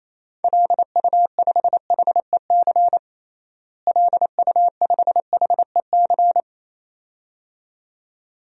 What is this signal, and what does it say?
Morse code, heard as a keyed tone, about 28 wpm, 700 Hz. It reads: LU5HEC LU5HEC
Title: DXCC-100 Call Signs - 6 Characters - Repeated 2x - 28wpm